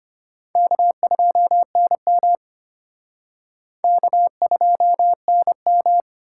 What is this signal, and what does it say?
Morse code, heard as a keyed tone, about 31 wpm, 700 Hz. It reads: K2NM K2NM
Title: Speed Racing - USA 1x2 Call Signs 25wpm